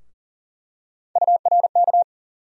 Morse code, heard as a keyed tone, about 40 wpm, 700 Hz. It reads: URK
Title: USA Call Sign Postfixes - 3 Letters 40wpm